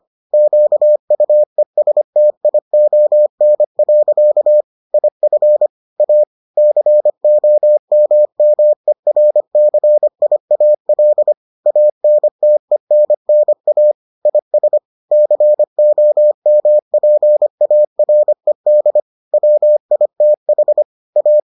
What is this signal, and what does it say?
Morse code, heard as a keyed tone, about 25 wpm, 600 Hz. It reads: QUESTION. IF A COMMERCIAL ANTENNA IS COMPARED WITH A